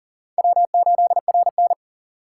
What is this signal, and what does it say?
Morse code, heard as a keyed tone, about 40 wpm, 700 Hz. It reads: W8RN
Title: USA Call Signs - 4 Characters - 3x Character Spacing - 40wpm